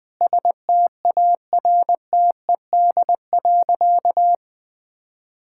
Morse code, heard as a keyed tone, about 21 wpm, 700 Hz. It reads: STARTED.
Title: Sentences with 5-Word Phrases - 20wpm